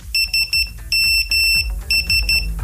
Morse code, heard as a keyed tone, about 13 wpm, 2700 Hz. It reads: SMS